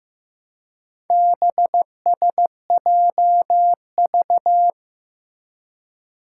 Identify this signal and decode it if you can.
Morse code, heard as a keyed tone, about 15 wpm, 700 Hz. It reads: BSJV
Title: Four Challenging Letter-Number Combinations 15wpm